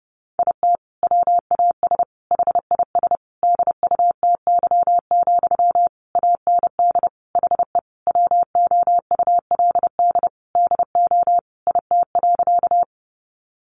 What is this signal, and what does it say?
Morse code, heard as a keyed tone, about 30 wpm, 700 Hz. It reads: IT WAS HIS DUTY, AND HE WOULD DO IT.
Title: Sentences from Top 1500 Words 30wpm